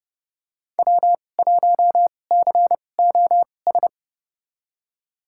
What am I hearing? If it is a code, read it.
Morse code, heard as a keyed tone, about 30 wpm, spaced as for 22 wpm, 700 Hz. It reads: W1COS
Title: USA Call Signs - 5 Characters - 2x Character Spacing - 30wpm